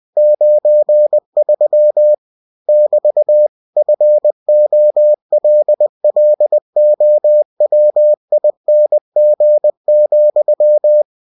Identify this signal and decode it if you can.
Morse code, heard as a keyed tone, about 20 wpm, 600 Hz. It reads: 93 <BT> FOLLOWING,
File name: cw-books / winnie-the-pooh / winnie-20wpm-page-0093